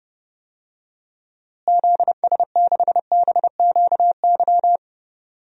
Morse code, heard as a keyed tone, about 30 wpm, 700 Hz. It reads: ZS6BQY
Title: DXCC-100 Call Signs - 6 Characters - Repeated 3x - 30wpm